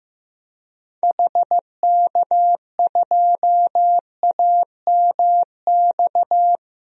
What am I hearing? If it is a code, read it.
Morse code, heard as a keyed tone, about 15 wpm, 700 Hz. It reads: HK2AMX
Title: DXCC-100 Call Signs - 6 Characters - Repeated 2x - 15wpm